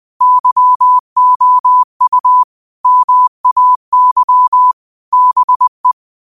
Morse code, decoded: YOU MAY BE

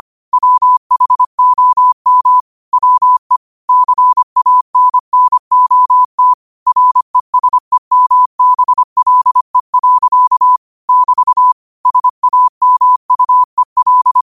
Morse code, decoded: WHOM WE CANNOT RESEMBLE. <BT> SAMUEL

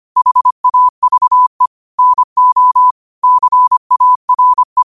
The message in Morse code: HAVE NO CARE

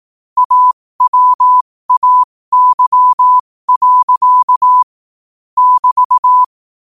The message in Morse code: AWAY. <BT>

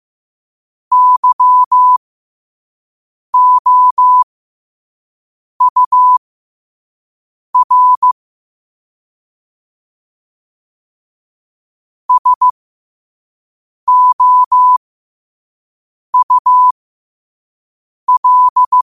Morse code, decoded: YOUR SOUL

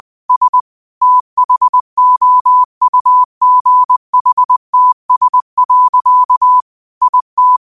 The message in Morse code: S THOUGHTS. IT

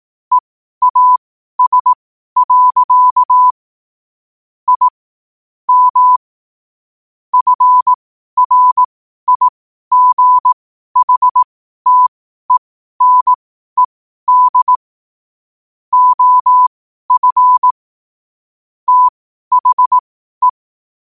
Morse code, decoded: EAS. I M FRIGHTENED OF THE

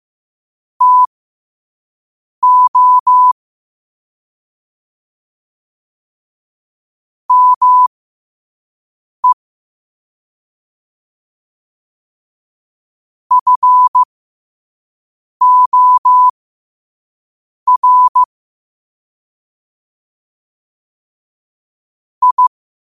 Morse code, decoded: TO ME FOR I

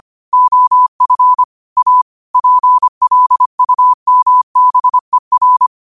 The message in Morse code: OF A PLUMBER

